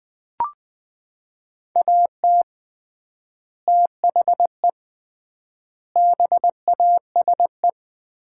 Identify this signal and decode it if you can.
Morse code, heard as a keyed tone, about 20 wpm, 700 Hz. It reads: AT THE BASE
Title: 3x Word-Spacing - Sets of 5 Words - 20wpm